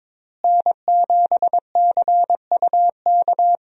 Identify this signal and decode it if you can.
Morse code, heard as a keyed tone, about 22 wpm, 700 Hz. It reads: N7CUK
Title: USA Call Signs - 5 Characters - Repeated 3x - 22wpm